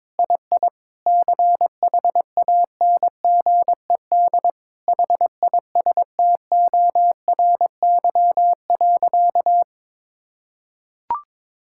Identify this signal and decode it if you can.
Morse code, heard as a keyed tone, about 21 wpm, 700 Hz. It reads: II CHANGED HISTORY.
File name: Sentences with 7-Word Phrases - 22wpm